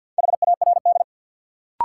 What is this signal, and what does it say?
Morse code, heard as a keyed tone, about 53 wpm, 700 Hz. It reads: HARD
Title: Sentences from Top 300 Words - New Word Drill - 50wpm